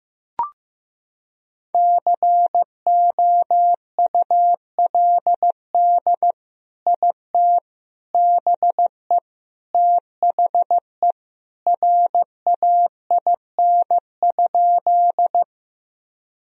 Morse code, heard as a keyed tone, about 15 wpm, 700 Hz. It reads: COULD IT BE THE RAIN?
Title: Sentences from Top 1000 Words - Review Part 1 - 15wpm